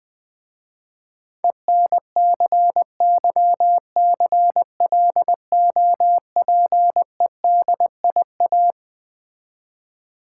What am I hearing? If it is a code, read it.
Morse code, heard as a keyed tone, about 21 wpm, 700 Hz. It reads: ENCYCLOPEDIA
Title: Difficult Words with Hints - 20wpm